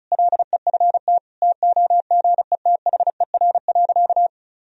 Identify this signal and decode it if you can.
Morse code, heard as a keyed tone, about 36 wpm, 700 Hz. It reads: LEFT TOGETHER.
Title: Sentences from Top 400 Words and Abbreviations 35wpm